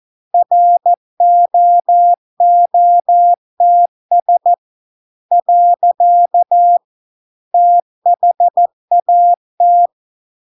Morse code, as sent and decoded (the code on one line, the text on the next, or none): .-. --- --- - ... / .-.-.- / - .... .- -
ROOTS . THAT